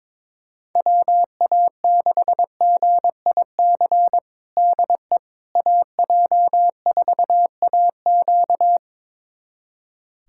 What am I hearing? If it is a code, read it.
Morse code, heard as a keyed tone, about 21 wpm, 700 Hz. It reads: WA6GIC DE AJ4AQ
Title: USA Callsign DE Callsign 22wpm